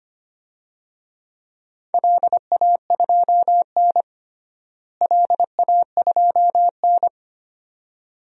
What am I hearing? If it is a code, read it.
Morse code, heard as a keyed tone, about 25 wpm, 700 Hz. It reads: LA2N LA2N
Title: DXCC-100 Call Signs - 4 Characters - Repeated 2x - 25wpm